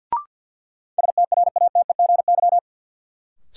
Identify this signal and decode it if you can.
Morse code, heard as a keyed tone, about 50 wpm, 700 Hz. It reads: STRATEGY
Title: Sentences from Top 1000 Words - Review Part 5 - New Word Drill - 50wpm